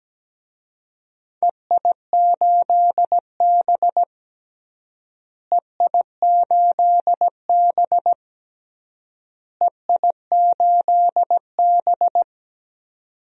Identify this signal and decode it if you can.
Morse code, heard as a keyed tone, about 17 wpm, 700 Hz. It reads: EI8B EI8B EI8B
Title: DXCC-100 Call Signs - 4 Characters - Repeated 3x - 17wpm